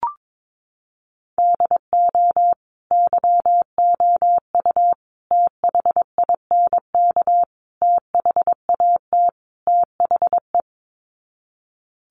Morse code, heard as a keyed tone, about 22 wpm, 700 Hz. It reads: DO YOU THINK THAT THE
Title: Sets of 5 Words Encore 22wpm